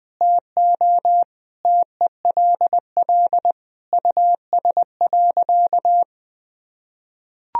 Morse code, heard as a keyed tone, about 21 wpm, 700 Hz. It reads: TO TELL US.
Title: Sentences from Top 1000 Words - Review Part 3 - 20wpm